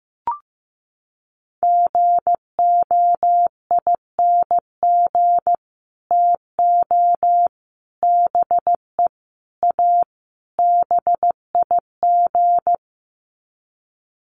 Morse code, heard as a keyed tone, about 15 wpm, 700 Hz. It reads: GOING TO BE A BIG
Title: Sets of 5 Words Encore2 15wpm